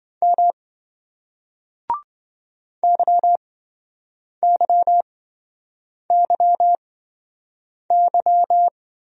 Morse code, heard as a keyed tone, about 26 wpm, spaced as for 7 wpm, 700 Hz. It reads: M YYYY